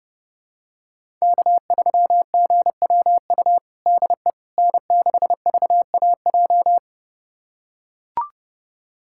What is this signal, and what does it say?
Morse code, heard as a keyed tone, about 32 wpm, 700 Hz. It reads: K3GWU DE N6VAJ
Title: USA Callsign DE Callsign - 5 Characters - 30wpm